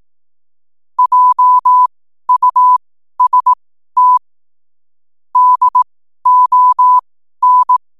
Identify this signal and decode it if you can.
Morse code, heard as a keyed tone, about 18 wpm, spaced as for 12 wpm, 1000 Hz. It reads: JUST DON
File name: n7net / 12wpm